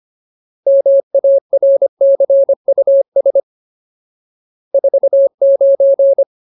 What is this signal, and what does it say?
Morse code, heard as a keyed tone, about 25 wpm, 550 Hz. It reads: MARCUS 49
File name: Ultimate Word List / 25WPM Ultimate